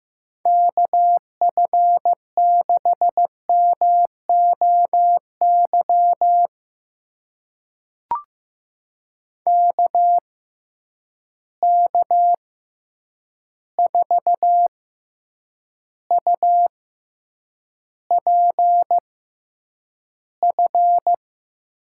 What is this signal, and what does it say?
Morse code, heard as a keyed tone, about 15 wpm, spaced as for 8 wpm, 700 Hz. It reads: KF6MOY KK4UPF
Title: USA Call Signs - 6 Characters - 6x Character Spacing - 15wpm